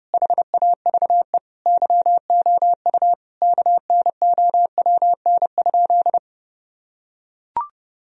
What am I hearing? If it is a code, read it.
Morse code, heard as a keyed tone, about 30 wpm, 700 Hz. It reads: HAVE YOU KNOWN?
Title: Sentences from Top 1000 Words - Review Part 3 - 30wpm